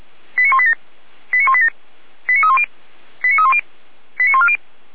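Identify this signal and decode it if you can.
Five-tone selective call (ZVEI), heard as quick runs of five tones, tones of about 70 ms. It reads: 89187 89187 89310 89310 89140
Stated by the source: ZVEI five-tone call 89229 received on SDR.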